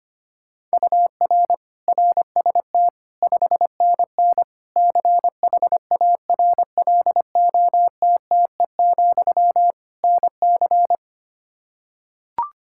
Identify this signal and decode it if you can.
Morse code, heard as a keyed tone, about 25 wpm, 700 Hz. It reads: UR RST 5NN CHARLOTTE, NC
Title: K1USN SST QSO Practice 25wpm